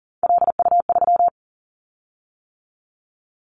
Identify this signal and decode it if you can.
Morse code, heard as a keyed tone, about 40 wpm, 700 Hz. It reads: LU3
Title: Callsign Prefixes for Top 100 DXCC - 40wpm